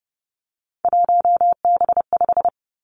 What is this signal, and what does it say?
Morse code, heard as a keyed tone, about 31 wpm, 700 Hz. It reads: 1B5
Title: Three Challenging Letter-Number Combinations 30wpm